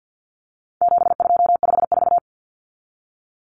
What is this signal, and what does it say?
Morse code, heard as a keyed tone, about 50 wpm, 700 Hz. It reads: 7254
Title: Numbers - 4-Digit 50wpm